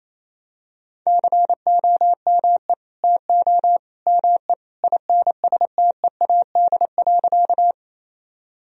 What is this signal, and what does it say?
Morse code, heard as a keyed tone, about 27 wpm, 700 Hz. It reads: COME TO ME INSTEAD.
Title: Sentences from Top 1000 Words - Review Part 4 - 28wpm